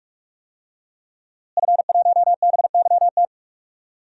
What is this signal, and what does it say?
Morse code, heard as a keyed tone, about 46 wpm, 700 Hz. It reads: F1BYT
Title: DXCC-100 Call Signs - 5 Characters - Repeated 3x - 45wpm